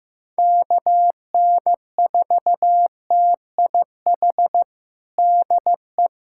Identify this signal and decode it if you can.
Morse code, heard as a keyed tone, about 15 wpm, 700 Hz. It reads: KN4TIH DE
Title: USA Callsign DE Callsign - 6 Characters - 15wpm